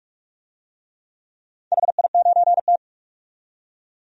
Morse code, heard as a keyed tone, about 44 wpm, 700 Hz. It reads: HI9T
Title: DXCC-100 Call Signs - 4 Characters - 45wpm